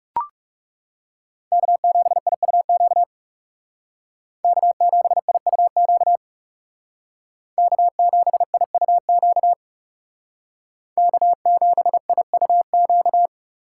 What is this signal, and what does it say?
Morse code, heard as a keyed tone, about 38 wpm, 700 Hz. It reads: K7IUQ K7IUQ K7IUQ K7IUQ